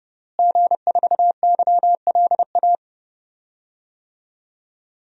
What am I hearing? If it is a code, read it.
Morse code, heard as a keyed tone, about 32 wpm, 700 Hz. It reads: G4YLA
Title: DXCC-100 Call Signs - 5 Characters - Repeated 3x - 30wpm